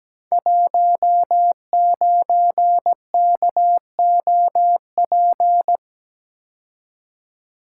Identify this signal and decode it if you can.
Morse code, heard as a keyed tone, about 17 wpm, 700 Hz. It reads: 19KOP
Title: Code Groups - 5 Letter:Numbers - 17wpm